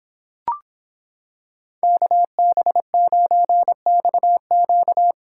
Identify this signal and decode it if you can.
Morse code, heard as a keyed tone, about 26 wpm, 700 Hz. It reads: KB9XQ